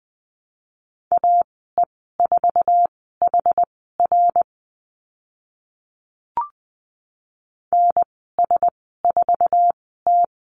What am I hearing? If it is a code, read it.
Morse code, heard as a keyed tone, about 20 wpm, spaced as for 14 wpm, 700 Hz. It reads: AE4HR NS4T
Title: 069 Lesson 12g - Letter d - Call Signs 20wpm